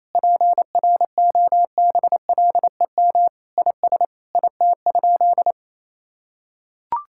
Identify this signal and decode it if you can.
Morse code, heard as a keyed tone, about 27 wpm, 700 Hz. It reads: PROBLEM IS IT?
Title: Sentences from Top 1000 Words - Review Part 6 - 28wpm